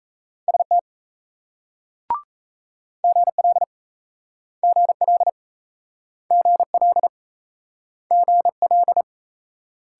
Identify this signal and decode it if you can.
Morse code, heard as a keyed tone, about 37 wpm, 700 Hz. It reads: ST GL GL GL GL